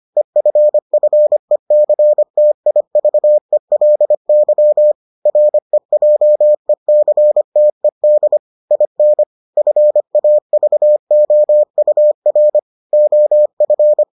Morse code, heard as a keyed tone, about 25 wpm, 600 Hz. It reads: EFFECTIVELY REJECTED IN FAVOUR OF